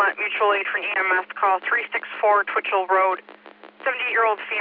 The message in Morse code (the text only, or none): none